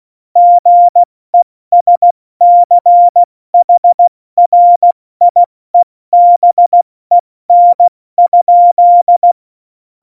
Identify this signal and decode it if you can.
Morse code, heard as a keyed tone, about 16 wpm, 700 Hz. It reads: GESCHRIEBEN?
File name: Morsekurs / Witze / Witz 10. September 2019, 70 F80